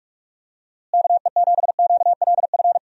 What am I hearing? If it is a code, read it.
Morse code, heard as a keyed tone, about 41 wpm, 700 Hz. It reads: KE7QLF
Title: USA Call Signs - 6 Characters - Repeated 2x - 45wpm